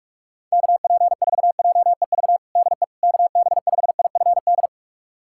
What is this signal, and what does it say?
Morse code, heard as a keyed tone, about 47 wpm, 690 Hz. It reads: KP4JEV DE KB5IFD
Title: USA Callsign DE Callsign 45wpm